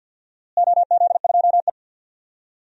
Morse code, heard as a keyed tone, about 53 wpm, 700 Hz. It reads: OZ2E
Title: DXCC-100 Call Signs - 4 Characters - Repeated 3x - 50wpm